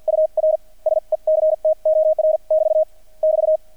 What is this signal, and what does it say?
Morse code, heard as a keyed tone, about 48 wpm, spaced as for 41 wpm, 630 Hz. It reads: UR SECTOR <BT> <BT>
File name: archived / CRSnet